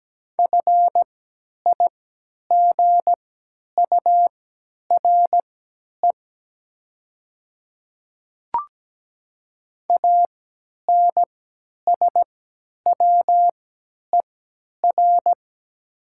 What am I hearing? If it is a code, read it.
Morse code, heard as a keyed tone, about 17 wpm, spaced as for 10 wpm, 700 Hz. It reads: FIGURE ANSWER
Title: 3x Character Spacing - Top 500 Words - 17wpm